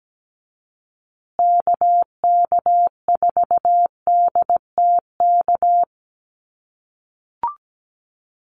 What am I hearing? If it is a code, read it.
Morse code, heard as a keyed tone, about 17 wpm, 700 Hz. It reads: KK4DTK
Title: USA Call Signs Repeated 2x 17wpm